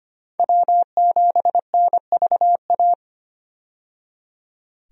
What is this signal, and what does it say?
Morse code, heard as a keyed tone, about 25 wpm, 690 Hz. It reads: W7NVA
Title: USA Call Signs - 5 Characters - 25wpm